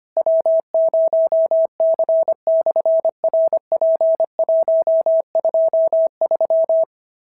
Morse code, heard as a keyed tone, about 25 wpm, 650 Hz. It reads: W0C/RP123
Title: SOTA QSO Practice 25wpm